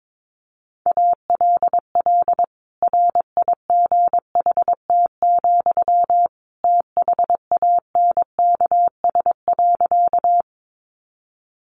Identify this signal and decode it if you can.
Morse code, heard as a keyed tone, about 22 wpm, 700 Hz. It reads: ALL RIGHT, THANKS.
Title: Sentences from Top 600 Words 22wpm